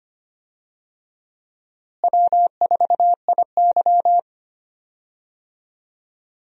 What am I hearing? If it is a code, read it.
Morse code, heard as a keyed tone, about 25 wpm, 700 Hz. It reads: W4IY
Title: USA Call Signs - 4 Characters - Repeated 3x - 25wpm